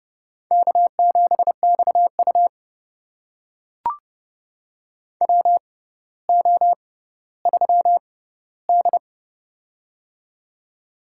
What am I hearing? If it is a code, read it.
Morse code, heard as a keyed tone, about 30 wpm, spaced as for 15 wpm, 700 Hz. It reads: K7XU WO3D
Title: USA Call Signs - 6x Character Spacing - 30wpm